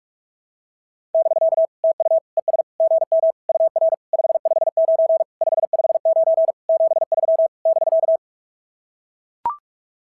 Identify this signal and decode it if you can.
Morse code, heard as a keyed tone, about 44 wpm, 660 Hz. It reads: <BK> TU ES GM UR 559 559 73 <BK>
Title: SOTA QSO Practice 45wpm